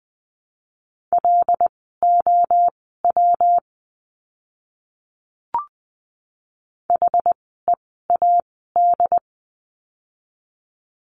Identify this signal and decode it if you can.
Morse code, heard as a keyed tone, about 20 wpm, spaced as for 14 wpm, 700 Hz. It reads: LOW HEAD
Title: 115 Lesson 19f - Letter w - Words 20wpm